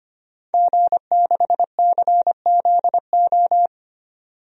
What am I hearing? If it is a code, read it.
Morse code, heard as a keyed tone, about 25 wpm, 700 Hz. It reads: G6CZO
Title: DXCC-100 Call Signs - 5 Characters - 25wpm